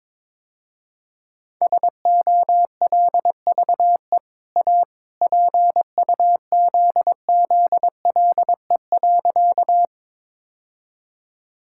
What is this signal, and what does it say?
Morse code, heard as a keyed tone, about 22 wpm, 700 Hz. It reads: SOLVE A PUZZLE.
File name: practice / 22wpm / Easy Sentences with Hints - 22wpm